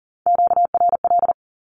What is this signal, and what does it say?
Morse code, heard as a keyed tone, about 40 wpm, 700 Hz. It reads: QRL
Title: Q-Codes with Spoken Meaning - 40wpm